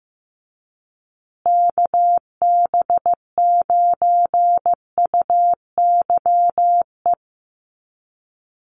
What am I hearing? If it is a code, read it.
Morse code, heard as a keyed tone, about 15 wpm, 700 Hz. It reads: KB9UYE
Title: USA Call Signs Repeated 3x 15wpm